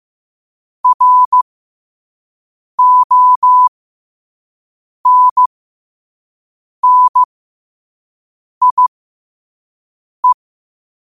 Morse code, heard as a keyed tone, about 15 wpm, spaced as for 5 wpm, 1000 Hz. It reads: RONNIE
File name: qotdfiles / 5wpm